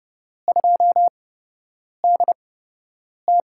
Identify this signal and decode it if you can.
Morse code, heard as a keyed tone, about 30 wpm, spaced as for 8 wpm, 700 Hz. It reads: 2DT
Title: USA Call Signs - 4 Characters - 8x Character Spacing - 30wpm